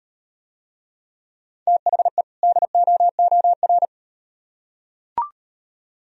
Morse code, heard as a keyed tone, about 38 wpm, 700 Hz. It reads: THE DOOR